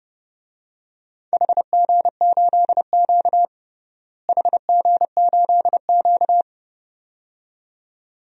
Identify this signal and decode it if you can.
Morse code, heard as a keyed tone, about 30 wpm, 700 Hz. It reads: HG8Q HG8Q
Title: DXCC-100 Call Signs - 4 Characters - Repeated 2x - 30wpm